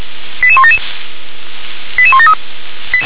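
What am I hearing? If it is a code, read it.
Five-tone selective call (ZVEI), heard as quick runs of five tones, tones of about 70 ms. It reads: 88177 88173
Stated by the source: Five-tone ZVEI selective call received on SDR.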